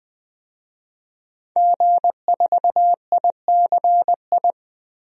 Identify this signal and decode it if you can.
Morse code, heard as a keyed tone, about 21 wpm, 700 Hz. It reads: G4ICI
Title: DXCC-100 Call Signs - 5 Characters - 20wpm